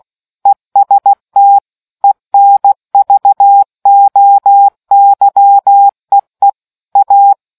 Morse code, heard as a keyed tone, about 16 wpm, 800 Hz. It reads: EST ENVOYEE A